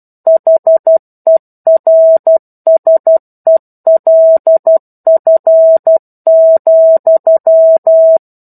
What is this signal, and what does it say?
Morse code, heard as a keyed tone, about 12 wpm, 650 Hz. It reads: HERSELF,
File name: 01-Chapter-01 12 WPM